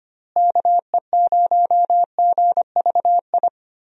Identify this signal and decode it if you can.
Morse code, heard as a keyed tone, about 25 wpm, 700 Hz. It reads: KE0GVI
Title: USA Call Signs - 6 Characters - Repeated 2x - 25wpm